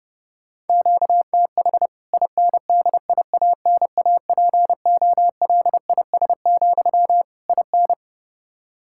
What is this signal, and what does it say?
Morse code, heard as a keyed tone, about 32 wpm, 700 Hz. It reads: QTH INDIANAPOLIS, IN
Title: K1USN SST QSO Practice 30wpm